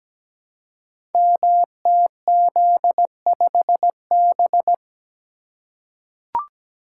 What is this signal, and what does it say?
Morse code, heard as a keyed tone, about 17 wpm, 700 Hz. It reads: MTZ5B
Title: Code Groups - 5 Letter:Numbers - 17wpm